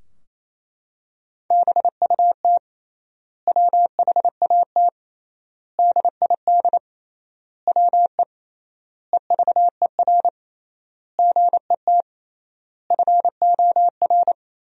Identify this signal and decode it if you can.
Morse code, heard as a keyed tone, about 28 wpm, 700 Hz. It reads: BUT WHAT DID WE EVER GET FOR
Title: 3x Word-Spacing - Sentences from Top 400 Words - 28wpm